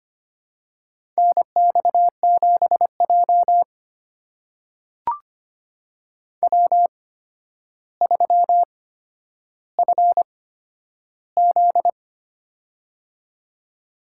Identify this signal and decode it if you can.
Morse code, heard as a keyed tone, about 25 wpm, spaced as for 11 wpm, 700 Hz. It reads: NX7J W3FZ
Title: USA Call Signs - 4 Characters - 8x Character Spacing - 25wpm